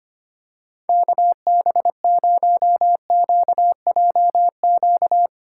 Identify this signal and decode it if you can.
Morse code, heard as a keyed tone, about 24 wpm, 700 Hz. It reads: KB0QJQ
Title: SOTA QSO Practice 25wpm